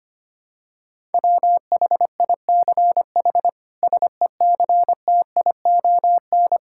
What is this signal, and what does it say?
Morse code, heard as a keyed tone, about 25 wpm, 700 Hz. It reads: WHICH SECTION
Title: Sentences from Top 1000 Words - Review Part 5 - 25wpm